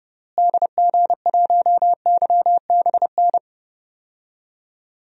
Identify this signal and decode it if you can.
Morse code, heard as a keyed tone, about 31 wpm, 700 Hz. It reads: DG1YBN
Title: DXCC-100 Call Signs - 6 Characters - Repeated 2x - 30wpm